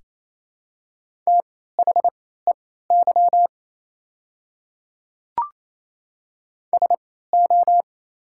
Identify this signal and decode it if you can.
Morse code, heard as a keyed tone, about 28 wpm, spaced as for 16 wpm, 700 Hz. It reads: THEY SO
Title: 3x Character Spacing - Top 100 Words - 28wpm